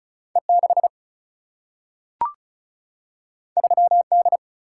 Morse code, heard as a keyed tone, about 36 wpm, 700 Hz. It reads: E 6 3 D
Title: Single-Letter Single-Number 35wpm